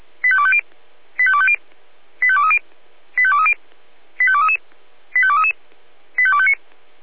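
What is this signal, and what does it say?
Five-tone selective call (ZVEI), heard as quick runs of five tones, tones of about 70 ms. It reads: 85250 85250 85230 85230 85233 85233 85259
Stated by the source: ZVEI five-tone selective call 85233, received by SDR.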